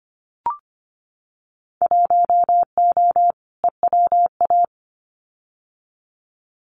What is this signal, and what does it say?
Morse code, heard as a keyed tone, about 25 wpm, 690 Hz. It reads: 1O EWA
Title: Winter Field Day Exchanges Encore 25wpm